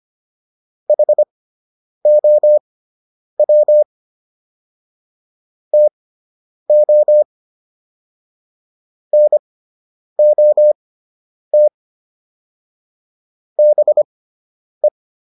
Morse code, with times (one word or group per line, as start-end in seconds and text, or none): 0.9-3.8 HOW
5.7-7.2 TO
9.1-11.7 NOT
13.6-14.9 BE